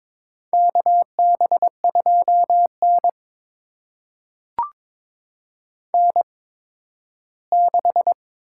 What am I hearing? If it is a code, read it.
Morse code, heard as a keyed tone, about 22 wpm, spaced as for 13 wpm, 700 Hz. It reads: KB2N N6
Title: USA Call Signs - 4 Characters - 8x Character Spacing - 22wpm